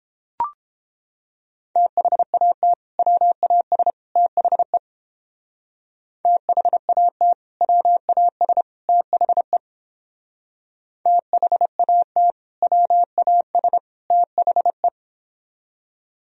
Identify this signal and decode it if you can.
Morse code, heard as a keyed tone, about 30 wpm, 700 Hz. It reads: THAT WAS THE THAT WAS THE THAT WAS THE